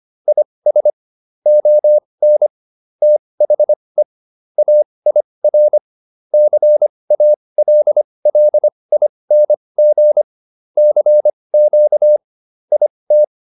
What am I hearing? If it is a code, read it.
Morse code, heard as a keyed tone, about 25 wpm, spaced as for 20 wpm, 600 Hz. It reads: IS ON THE AIR CALLING CQ IT